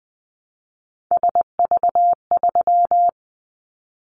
Morse code, heard as a keyed tone, about 20 wpm, 700 Hz. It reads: SV3
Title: Callsign Prefixes for Top 50 DXCC - 20wpm